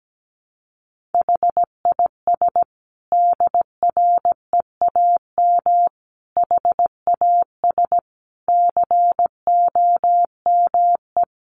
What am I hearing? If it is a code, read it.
Morse code, heard as a keyed tone, about 17 wpm, 700 Hz. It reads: HIS DREAM HAS COME
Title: Sentences from Top 1500 Words 17wpm